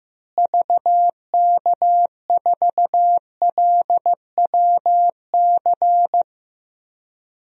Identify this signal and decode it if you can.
Morse code, heard as a keyed tone, about 15 wpm, 700 Hz. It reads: VK4LWC
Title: DXCC-100 Call Signs - 6 Characters - Repeated 3x - 15wpm